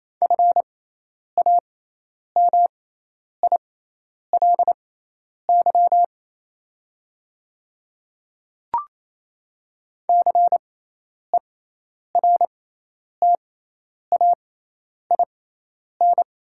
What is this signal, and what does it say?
Morse code, heard as a keyed tone, about 28 wpm, spaced as for 10 wpm, 700 Hz. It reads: FAMILY CERTAIN
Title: 6x Character Spacing - Top 500 Words - 28wpm